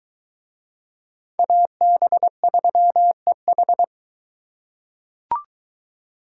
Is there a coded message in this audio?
Yes: Morse code from 1.4 to 3.8 s, heard as a keyed tone, about 23 wpm, 700 Hz. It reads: AB3EH